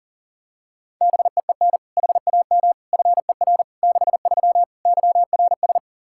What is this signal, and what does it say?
Morse code, heard as a keyed tone, about 40 wpm, 700 Hz. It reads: BEEN HAM FER 63 YRS
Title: Practice QSO - Exchange 3 - More Info - 40wpm